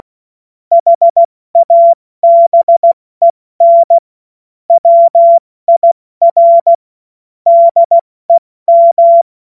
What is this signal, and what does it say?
Morse code, heard as a keyed tone, about 16 wpm, 680 Hz. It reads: HABEN WIR DEM
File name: Morsekurs / Witze / Witz 19. Oktober 2021, 70 F80